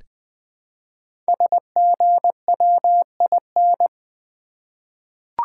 Morse code, heard as a keyed tone, about 20 wpm, 700 Hz. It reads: SGWIN
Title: Code Groups - 5 Letters - 20wpm